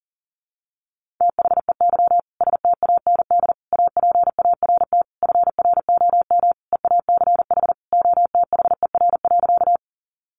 Morse code, heard as a keyed tone, about 40 wpm, 700 Hz. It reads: THEY STAND APART FROM EACH OTHER.
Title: Easy Phrases 40wpm